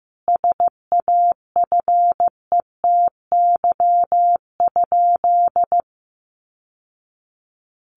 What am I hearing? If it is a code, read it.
Morse code, heard as a keyed tone, about 15 wpm, 700 Hz. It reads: SAFETY?
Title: Sentences from Top 700 Words 15wpm